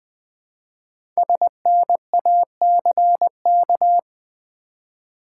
Morse code, heard as a keyed tone, about 20 wpm, 700 Hz. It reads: SNACK
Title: Easy Words with Hints - 20wpm